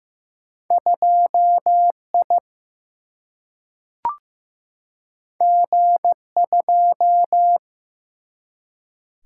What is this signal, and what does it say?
Morse code, heard as a keyed tone, about 15 wpm, 700 Hz. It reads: 2 I G 2
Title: Single-Letter Single-Number 15wpm